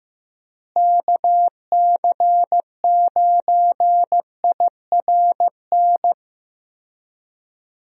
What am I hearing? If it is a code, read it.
Morse code, heard as a keyed tone, about 15 wpm, 700 Hz. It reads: KC9IRN
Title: USA Call Signs - 6 Characters - Repeated 3x - 15wpm